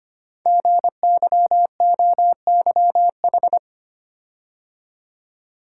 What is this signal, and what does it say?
Morse code, heard as a keyed tone, about 25 wpm, 700 Hz. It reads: GYOYH
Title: Code Groups - 5 Letter:Numbers - 25wpm